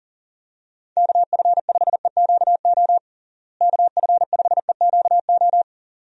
Morse code, heard as a keyed tone, about 40 wpm, 700 Hz. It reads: KF5EQO KF5EQO
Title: USA Call Signs - 6 Characters - Repeated 2x - 40wpm